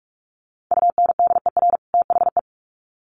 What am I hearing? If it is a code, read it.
Morse code, heard as a keyed tone, about 44 wpm, 700 Hz. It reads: UNDER THE
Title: Sets of 2 Words 45wpm